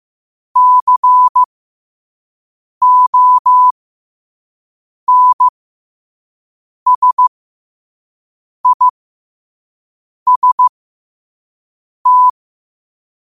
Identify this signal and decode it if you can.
Morse code, heard as a keyed tone, about 15 wpm, spaced as for 5 wpm, 1000 Hz. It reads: CONSIST
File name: qotdfiles / 5wpm